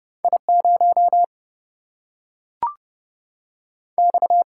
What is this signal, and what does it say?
Morse code, heard as a keyed tone, about 30 wpm, 700 Hz. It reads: I 0 X
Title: Single-Letter Single-Number 30wpm